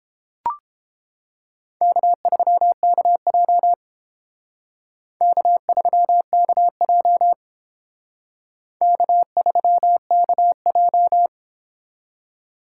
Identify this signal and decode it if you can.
Morse code, heard as a keyed tone, about 29 wpm, 700 Hz. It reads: K3KJ K3KJ K3KJ